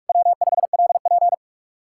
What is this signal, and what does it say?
Morse code, heard as a keyed tone, about 62 wpm, spaced as for 53 wpm, 700 Hz. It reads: W5LP
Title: Speed-Racing Common QSO Phrases - 45wpm